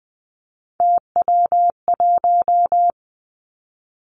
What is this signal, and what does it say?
Morse code, heard as a keyed tone, about 20 wpm, 700 Hz. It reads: TW1
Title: Three Challenging Letter-Number Combinations 20wpm